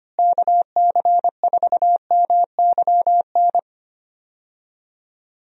Morse code, heard as a keyed tone, about 25 wpm, 700 Hz. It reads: KC4MYN
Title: USA Call Signs - 6 Characters - Repeated 2x - 25wpm